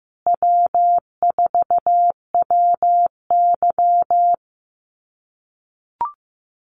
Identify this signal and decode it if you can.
Morse code, heard as a keyed tone, about 15 wpm, 700 Hz. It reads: W4WY
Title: USA Call Signs Repeated 3x 15wpm